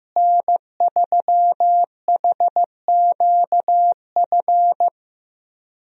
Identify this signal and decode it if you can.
Morse code, heard as a keyed tone, about 15 wpm, 700 Hz. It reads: N3HQF
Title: USA Call Signs - 5 Characters - Repeated 2x - 15wpm